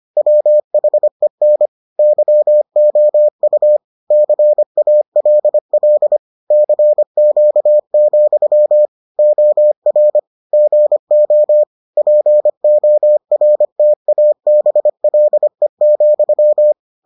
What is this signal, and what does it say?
Morse code, heard as a keyed tone, about 25 wpm, 600 Hz. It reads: WHEN YOU CALL CQ, OR GO PORTABLE,